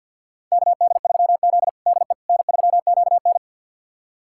Morse code, heard as a keyed tone, about 50 wpm, 700 Hz. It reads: KD3Z DE N3QN
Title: USA Callsign DE Callsign - 4 Characters - 50wpm